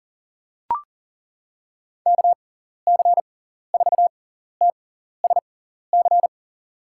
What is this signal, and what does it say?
Morse code, heard as a keyed tone, about 38 wpm, spaced as for 14 wpm, 700 Hz. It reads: KC4TSC
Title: USA Call Signs - 6 Characters - 6x Character Spacing - 40wpm